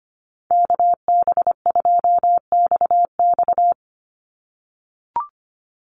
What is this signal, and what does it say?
Morse code, heard as a keyed tone, about 25 wpm, 700 Hz. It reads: KB2XX
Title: NA SOTA Chaser Callsigns 25wpm